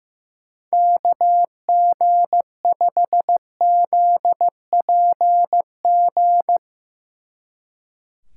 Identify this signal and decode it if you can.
Morse code, heard as a keyed tone, about 15 wpm, 700 Hz. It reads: KG5ZPG
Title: Speed Racing - USA 2x3 Call Signs 15wpm